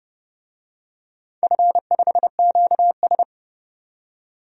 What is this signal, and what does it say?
Morse code, heard as a keyed tone, about 28 wpm, 700 Hz. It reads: F5QS
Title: DXCC-100 Call Signs - 4 Characters - 30wpm